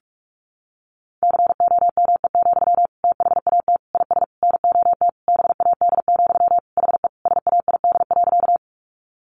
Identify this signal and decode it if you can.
Morse code, heard as a keyed tone, about 44 wpm, 700 Hz. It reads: COME, THAT IS NOT BAD, HE SAID.
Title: Sentences from Top 700 Words 45wpm